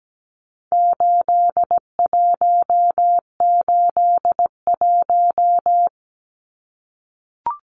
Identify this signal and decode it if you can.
Morse code, heard as a keyed tone, about 17 wpm, 700 Hz. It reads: 8181
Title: Numbers - 4-Digit 17wpm